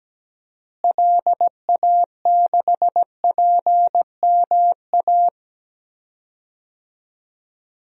Morse code, heard as a keyed tone, about 17 wpm, 700 Hz. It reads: LA6PMA
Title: DXCC-100 Call Signs - 6 Characters - Repeated 2x - 17wpm